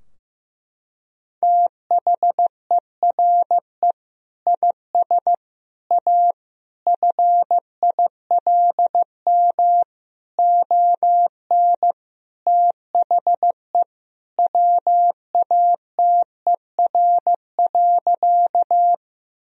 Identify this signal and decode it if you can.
Morse code, heard as a keyed tone, about 15 wpm, 700 Hz. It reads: THERE IS A FILM ON THE WATER.
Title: Sentences from Top 1000 Words - Review Part 3 - 15wpm